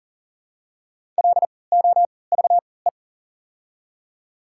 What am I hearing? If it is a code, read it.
Morse code, heard as a keyed tone, about 40 wpm, spaced as for 23 wpm, 700 Hz. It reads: LOVE
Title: 3x Character Spacing - Top 500 Words - 40wpm